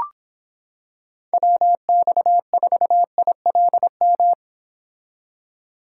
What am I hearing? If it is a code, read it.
Morse code, heard as a keyed tone, about 26 wpm, 700 Hz. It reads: WX4ILM